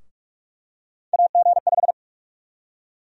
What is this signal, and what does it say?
Morse code, heard as a keyed tone, about 42 wpm, 700 Hz. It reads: AG5
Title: USA Call Sign Prefixes - 45wpm